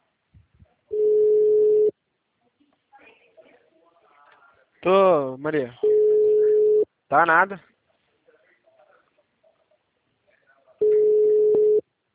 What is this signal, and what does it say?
Ringback tone (Europe, 425 Hz), with speech.